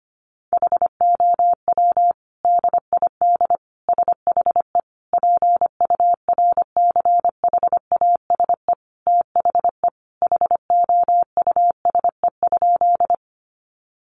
Morse code, heard as a keyed tone, about 25 wpm, 700 Hz. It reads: HOW DID SHE PURCHASE THE HOUSE?
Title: Sentences from Top 800 Words 25wpm